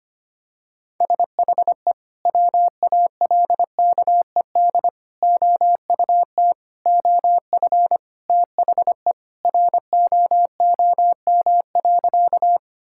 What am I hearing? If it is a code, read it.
Morse code, heard as a keyed tone, about 25 wpm, 700 Hz. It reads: SHE WALKED OUT OF THE ROOM.
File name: Sentences with 3-Word Phrases - 25wpm